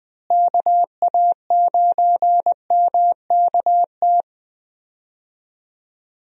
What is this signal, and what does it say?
Morse code, heard as a keyed tone, about 20 wpm, 700 Hz. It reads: KA9MKT
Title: USA Call Signs - 6 Characters - Repeated 2x - 20wpm